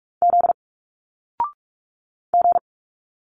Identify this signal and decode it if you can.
Morse code, heard as a keyed tone, about 46 wpm, 700 Hz. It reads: Z G
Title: Single Letters 45wpm